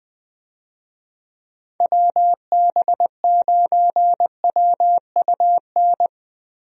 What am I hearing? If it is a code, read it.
Morse code, heard as a keyed tone, about 20 wpm, 700 Hz. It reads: WB9WUN
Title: USA Call Signs - 6 Characters - Repeated 2x - 20wpm